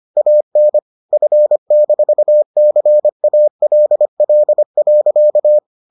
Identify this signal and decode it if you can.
Morse code, heard as a keyed tone, about 25 wpm, 600 Hz. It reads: AN F-CALL.